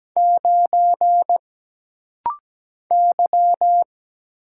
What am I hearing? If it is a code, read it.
Morse code, heard as a keyed tone, about 17 wpm, 700 Hz. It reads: 9 Y
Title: Single Letter-Number Alt - Rapid-Fire 17wpm